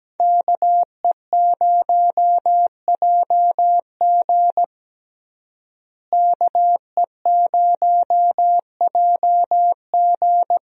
Morse code, heard as a keyed tone, about 17 wpm, 700 Hz. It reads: KE0JG KE0JG
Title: USA Call Signs - 5 Characters - Repeated 3x - 17wpm